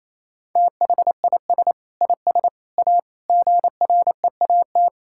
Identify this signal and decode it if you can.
Morse code, heard as a keyed tone, about 28 wpm, 700 Hz. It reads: THIS IS A GREAT
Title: Sentences from Top 1000 Words - Review Part 6 - 28wpm